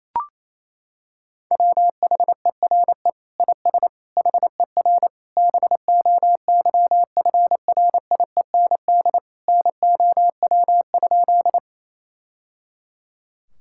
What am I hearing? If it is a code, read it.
Morse code, heard as a keyed tone, about 27 wpm, 700 Hz. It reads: WHERE IS HER BOYFRIEND NOW?
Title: Sentences from Top 1000 Words - Review Part 5 - 28wpm